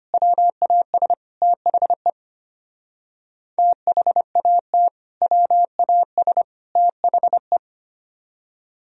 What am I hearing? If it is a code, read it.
Morse code, heard as a keyed tone, about 32 wpm, 700 Hz. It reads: WAS THE THAT WAS THE